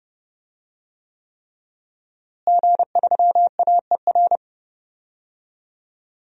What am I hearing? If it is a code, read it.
Morse code, heard as a keyed tone, about 30 wpm, 700 Hz. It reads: G3AER
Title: DXCC-100 Call Signs - 5 Characters - Repeated 3x - 30wpm